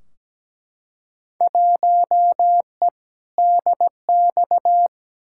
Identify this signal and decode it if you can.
Morse code, heard as a keyed tone, about 17 wpm, 700 Hz. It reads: 1E DX
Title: Field Day Exchanges Encore - 17wpm